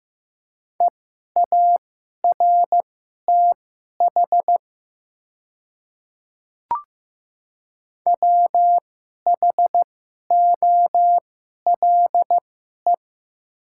Morse code, heard as a keyed tone, about 15 wpm, spaced as for 11 wpm, 700 Hz. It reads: EARTH WHOLE
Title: 2x Character Spacing - Top 500 Words - 15wpm